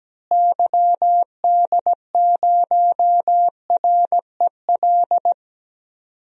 Morse code, heard as a keyed tone, about 17 wpm, 700 Hz. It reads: YD0REL
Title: DXCC-100 Call Signs - 6 Characters - Repeated 2x - 17wpm